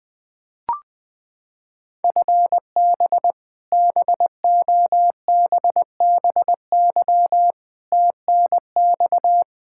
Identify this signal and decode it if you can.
Morse code, heard as a keyed tone, about 19 wpm, 700 Hz. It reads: FB BOBBY TNX
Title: Practice QSO - Exchange 3 - More Info - 20wpm